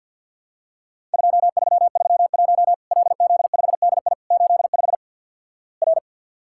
Morse code, heard as a keyed tone, about 50 wpm, 700 Hz. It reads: 2331 LZ5DI 85 R
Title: Worked All Europe DX Contest - QTC Exchange with Repeats - 50wpm